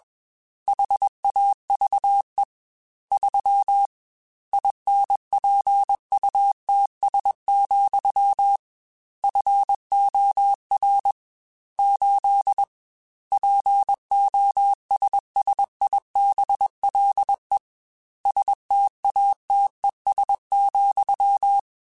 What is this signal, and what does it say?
Morse code, heard as a keyed tone, about 21 wpm, 800 Hz. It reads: HAVE 3 INPUTS, FOR 8 POSSIBLE STATES,
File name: Technical Subjects / Technical Subjects-mp3s / AND Gate @20